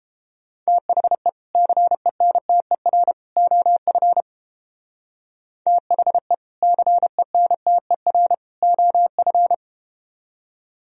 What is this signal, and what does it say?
Morse code, heard as a keyed tone, about 30 wpm, 700 Hz. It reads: THE CENTER OF THE CENTER OF